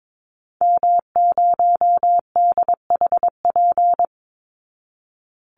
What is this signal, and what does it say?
Morse code, heard as a keyed tone, about 22 wpm, 700 Hz. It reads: M0DHP
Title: Call Signs Repeated 2x - 22wpm